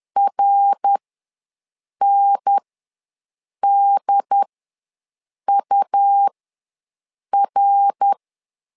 Morse code, heard as a keyed tone, about 11 wpm, spaced as for 6 wpm, 800 Hz. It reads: RNDUR
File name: LEZIONI / 3 MWRO / 3 MWR ws 10wpm 2151